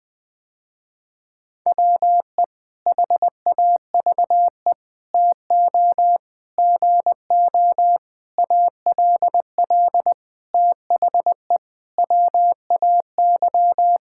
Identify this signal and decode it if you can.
Morse code, heard as a keyed tone, about 20 wpm, 700 Hz. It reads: WE HAVE TO GO ALL THE WAY
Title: Sentences with 8-Word Phrases - Spoken Beginning - 20wpm